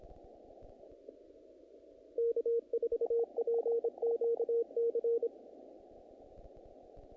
Morse code, heard as a keyed tone, about 26 wpm, 470 Hz. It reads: K4PQC
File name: contests / naqp-cw-k5tr-op-k5pi- -1900